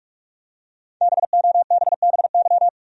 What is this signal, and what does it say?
Morse code, heard as a keyed tone, about 46 wpm, 700 Hz. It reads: BOBBY
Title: USA Common Names - Max 5 Letters - 45wpm